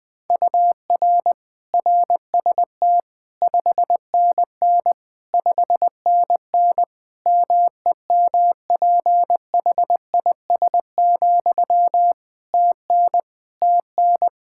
Morse code, heard as a keyed tone, about 20 wpm, 700 Hz. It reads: UR RST 5NN 5NN MEMPHIS, TN TN
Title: Practice QSO - Exchange 1 - Beginning - 20wpm